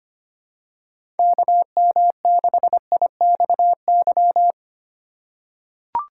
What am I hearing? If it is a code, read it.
Morse code, heard as a keyed tone, about 25 wpm, 700 Hz. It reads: KM6IXY
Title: USA Call Signs - 6 Characters - 6x Character Spacing - 25wpm